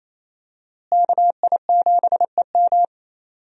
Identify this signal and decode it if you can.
Morse code, heard as a keyed tone, about 27 wpm, 700 Hz. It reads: KI7EM
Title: USA Call Signs - 5 Characters - Repeated 2x - 28wpm